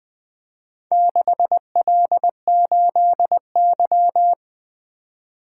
Morse code, heard as a keyed tone, about 20 wpm, 700 Hz. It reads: 6L8Y
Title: Four Letter-Number Combinations - 20wpm